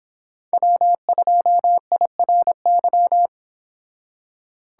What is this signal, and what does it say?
Morse code, heard as a keyed tone, about 26 wpm, 700 Hz. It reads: W2IRY